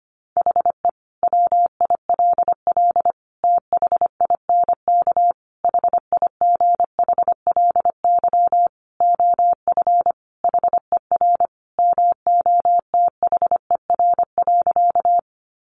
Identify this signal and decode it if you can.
Morse code, heard as a keyed tone, about 25 wpm, 700 Hz. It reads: HE WILL THINK HIGHLY OF HER MOTHER.
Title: Sentences from Top 1100 Words 25wpm